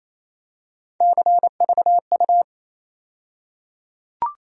Morse code, heard as a keyed tone, about 28 wpm, 700 Hz. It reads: CVU
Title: USA Call Sign Postfixes - 3 Letters 28wpm